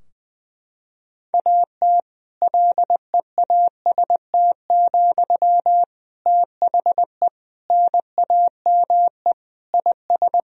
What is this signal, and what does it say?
Morse code, heard as a keyed tone, about 20 wpm, 700 Hz. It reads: AT LEAST, THE NAME IS
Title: Sentences from Top 1000 Words - Review Part 4 - 20wpm